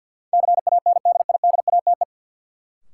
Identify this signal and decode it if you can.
Morse code, heard as a keyed tone, about 47 wpm, 700 Hz. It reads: CANDIDATE
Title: Sentences from Top 1600 Words - New Word Drill Alt - 50wpm